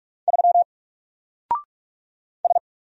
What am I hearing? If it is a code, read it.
Morse code, heard as a keyed tone, about 46 wpm, 700 Hz. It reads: 3 S
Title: Single Letter-Number 45wpm